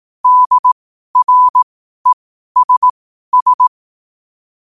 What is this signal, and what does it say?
Morse code, heard as a keyed tone, about 18 wpm, spaced as for 12 wpm, 1000 Hz. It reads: DRESS
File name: qotdfiles / 12wpm